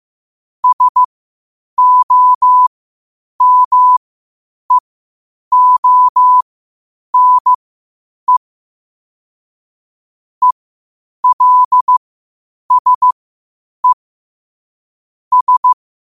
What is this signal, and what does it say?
Morse code, heard as a keyed tone, about 15 wpm, spaced as for 8 wpm, 1000 Hz. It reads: SOMEONE ELSE S
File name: qotdfiles / 8wpm